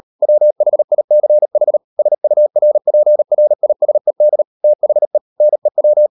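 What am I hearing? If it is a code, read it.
Morse code, heard as a keyed tone, about 38 wpm, 600 Hz. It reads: WHICH SURPRISED THE NEW
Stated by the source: Morse code practice at 38 WPM, sending Ken Follett's novel Never in English, from CW med Gnister.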